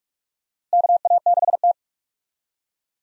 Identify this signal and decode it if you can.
Morse code, heard as a keyed tone, about 46 wpm, 700 Hz. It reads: KA6T
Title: USA Call Signs - 4 Characters - Repeated 2x - 45wpm